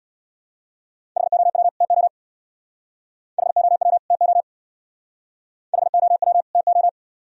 Morse code, heard as a keyed tone, about 63 wpm, 700 Hz. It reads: HOW TO HOW TO HOW TO